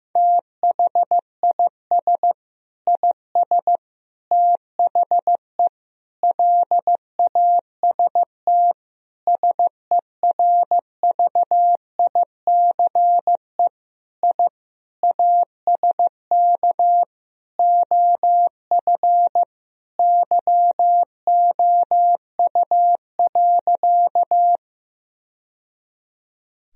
Morse code, heard as a keyed tone, about 15 wpm, 700 Hz. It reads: THIS IS THE LAST SERVICE I ASK OF YOU.
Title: Sentences from Top 1000 Words - Review Part 4 - 15wpm